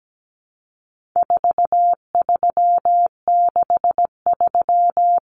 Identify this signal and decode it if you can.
Morse code, heard as a keyed tone, about 17 wpm, 700 Hz. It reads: 4363
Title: Common Cut Numbers - 4-Digit 17wpm